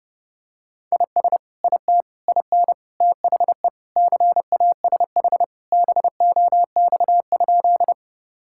Morse code, heard as a keyed tone, about 32 wpm, 700 Hz. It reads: IS IT IN THE CASH BOX?
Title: Sentences from Top 1000 Words - Review Part 5 - 30wpm